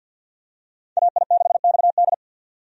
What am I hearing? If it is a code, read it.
Morse code, heard as a keyed tone, about 50 wpm, 700 Hz. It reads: AI6XD
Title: USA Call Signs - 5 Characters - 50wpm